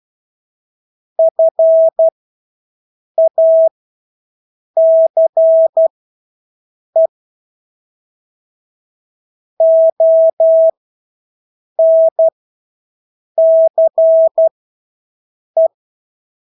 Morse code, heard as a keyed tone, about 12 wpm, spaced as for 6 wpm, 650 Hz. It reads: FACE ONCE
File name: Engelske ord 12-6 wpm